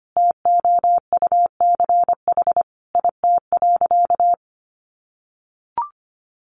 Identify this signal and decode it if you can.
Morse code, heard as a keyed tone, about 25 wpm, 700 Hz. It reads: TOUCH IT.
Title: Sentences from Top 1200 Words 25wpm